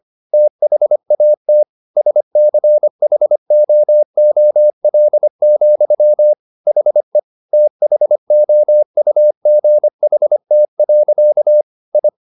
Morse code, heard as a keyed tone, about 25 wpm, 600 Hz. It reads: THAT SCHOOL, HE THOUGHT. I
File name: the old man and the sea part 1 25wpm